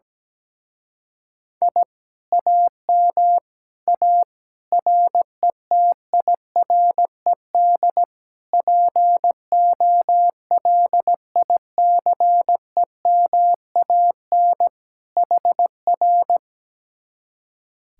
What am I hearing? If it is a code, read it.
Morse code, heard as a keyed tone, about 17 wpm, 700 Hz. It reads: I AM A RETIRED POLICEMAN HR
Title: Speed-Racing Common QSO Phrases Encore - 17wpm